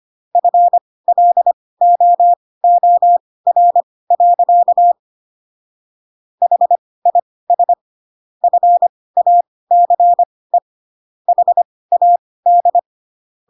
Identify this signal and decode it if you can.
Morse code, heard as a keyed tone, about 25 wpm, spaced as for 18 wpm, 700 Hz. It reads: FLOOR. HIS FACE HAD
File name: Pinocchio / Pinocchio ch1 12 25 ik1hge